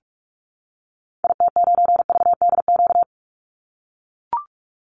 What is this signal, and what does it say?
Morse code, heard as a keyed tone, about 44 wpm, 700 Hz. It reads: IT9VDQ
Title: Call Signs - 45wpm